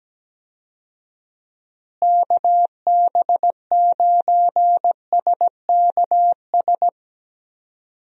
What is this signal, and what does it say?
Morse code, heard as a keyed tone, about 17 wpm, 700 Hz. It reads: KB9SKS
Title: USA Call Signs - 6 Characters - Repeated 2x - 17wpm